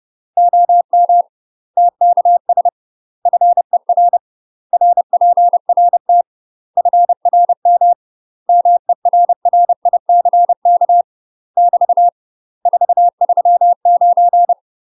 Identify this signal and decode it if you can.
Morse code, heard as a keyed tone, about 30 wpm, 700 Hz. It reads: OM TKS FER RPRT FRM MERRICK <BT> 439
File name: UK MASTER QSOs TRAINING / 30WPM / UK MASTER QSOs 0043